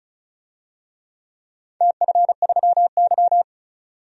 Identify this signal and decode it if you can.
Morse code, heard as a keyed tone, about 36 wpm, 690 Hz. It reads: TF3Y
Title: DXCC-100 Call Signs - 4 Characters - 35wpm